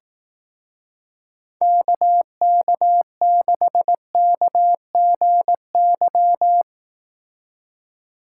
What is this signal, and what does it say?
Morse code, heard as a keyed tone, about 18 wpm, 700 Hz. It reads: KK6KGY
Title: Speed Racing - USA 2x3 Call Signs 15wpm